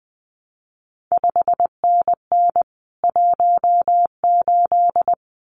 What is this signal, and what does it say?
Morse code, heard as a keyed tone, about 20 wpm, 700 Hz. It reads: 5NN 18
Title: CQ WW DX Contest Exchange Encore - 20wpm